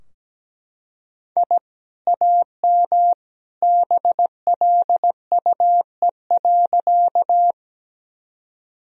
Morse code, heard as a keyed tone, about 17 wpm, 700 Hz. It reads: I AM BLUE.
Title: Sentences from Top 1000 Words - Review Part 3 - 17wpm